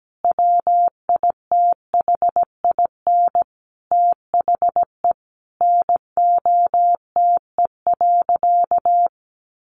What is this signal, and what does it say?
Morse code, heard as a keyed tone, about 17 wpm, 700 Hz. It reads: WITHIN THE NOTE.
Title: Sentences from Top 1500 Words 17wpm